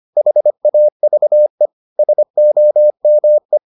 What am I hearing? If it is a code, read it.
Morse code, heard as a keyed tone, about 25 wpm, 600 Hz. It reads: HAVE SOME